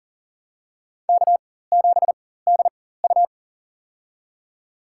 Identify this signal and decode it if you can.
Morse code, heard as a keyed tone, about 37 wpm, spaced as for 18 wpm, 700 Hz. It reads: K7DU
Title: USA Call Signs - 4 Characters - 4x Character Spacing - 40wpm